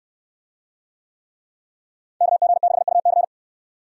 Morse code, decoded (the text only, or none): KG6UC